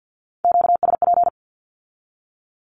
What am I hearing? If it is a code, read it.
Morse code, heard as a keyed tone, about 50 wpm, 700 Hz. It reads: QSP
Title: Q-Codes with Spoken Meaning - 50wpm